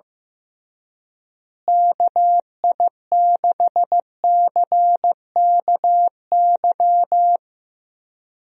Morse code, heard as a keyed tone, about 15 wpm, 700 Hz. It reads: KI6CKY
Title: USA Call Signs - 6 Characters - Repeated 3x - 15wpm